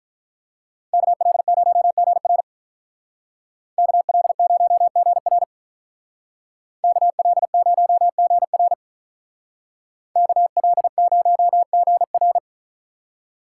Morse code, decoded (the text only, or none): KL0GR KL0GR KL0GR KL0GR